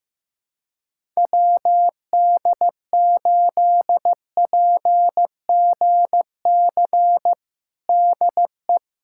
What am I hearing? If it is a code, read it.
Morse code, heard as a keyed tone, about 15 wpm, 700 Hz. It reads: WD8PGC DE
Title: USA Callsign DE Callsign - 6 Characters - 15wpm